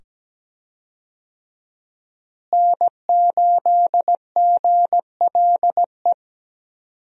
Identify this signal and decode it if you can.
Morse code, heard as a keyed tone, about 17 wpm, 700 Hz. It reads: N8GLE
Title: USA Call Signs - 5 Characters - Repeated 2x - 17wpm